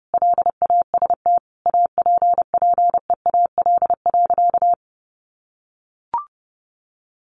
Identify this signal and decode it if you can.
Morse code, heard as a keyed tone, about 30 wpm, 700 Hz. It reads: LAST APPEAL.
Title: Sentences from Top 1300 Words 30wpm